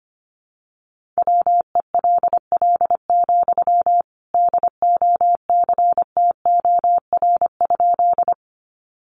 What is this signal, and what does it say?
Morse code, heard as a keyed tone, about 25 wpm, 700 Hz. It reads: WELL, DOCTOR?
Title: Sentences from Top 1500 Words 25wpm